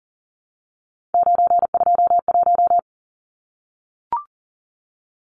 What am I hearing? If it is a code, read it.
Morse code, heard as a keyed tone, about 40 wpm, 700 Hz. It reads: 921
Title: Numbers - 3-Digit 40wpm